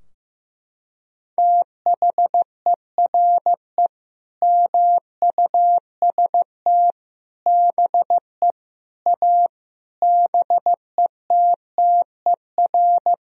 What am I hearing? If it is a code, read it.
Morse code, heard as a keyed tone, about 15 wpm, 700 Hz. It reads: THERE MUST BE A BETTER
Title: Sentences from Top 1000 Words - Review Part 4 - 15wpm